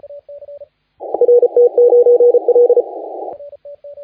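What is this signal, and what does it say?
Morse code, heard as a keyed tone, about 34 wpm, 480 Hz. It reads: RT9L
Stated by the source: Morse contest audio recorded by LA4C during the SAC CW 2018 contest.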